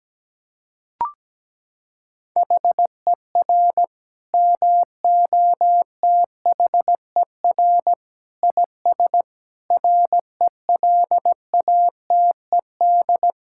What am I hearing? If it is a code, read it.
Morse code, heard as a keyed tone, about 17 wpm, 700 Hz. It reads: HER MOTHER IS RELATED
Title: Sentences from Top 1000 Words - Review Part 5 - 17wpm